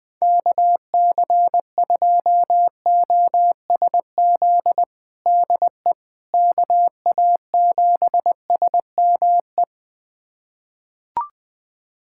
Morse code, heard as a keyed tone, about 21 wpm, 700 Hz. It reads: KC2OSZ DE KA7SME
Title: USA Callsign DE Callsign - 6 Characters - 20wpm